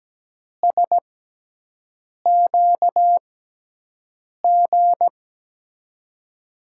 Morse code, heard as a keyed tone, about 17 wpm, spaced as for 6 wpm, 700 Hz. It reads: SQG
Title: USA Call Signs - 6 Characters - 6x Character Spacing - 17wpm